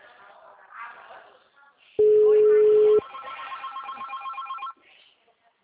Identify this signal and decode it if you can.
Ringback tone (Europe, 425 Hz).
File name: out- -2041- - - .60761